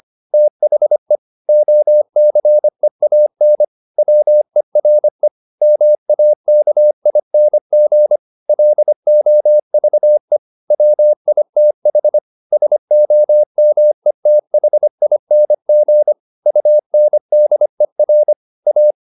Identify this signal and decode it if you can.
Morse code, heard as a keyed tone, about 25 wpm, 600 Hz. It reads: THE OCEAN WERE MAKING LOVE WITH SOMETHING UNDER A